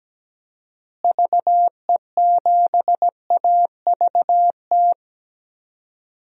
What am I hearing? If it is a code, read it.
Morse code, heard as a keyed tone, about 17 wpm, 700 Hz. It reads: VE7AVT
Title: DXCC-100 Call Signs - 6 Characters - Repeated 3x - 17wpm